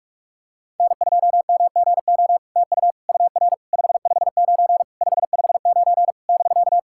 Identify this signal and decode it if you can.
Morse code, heard as a keyed tone, about 43 wpm, 700 Hz. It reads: N2MGO TU UR 559 559 <BK>
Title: SOTA QSO Practice 45wpm